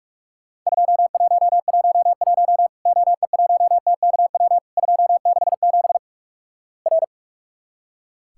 Morse code, heard as a keyed tone, about 46 wpm, 700 Hz. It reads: 2111 OE1TKW 267 R
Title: Worked All Europe DX Contest - QTC Exchange - 45wpm